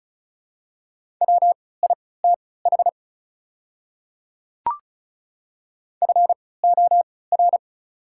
Morse code, heard as a keyed tone, about 36 wpm, spaced as for 20 wpm, 690 Hz. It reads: WITH FOR
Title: 3x Character Spacing - Top 100 Words - 35wpm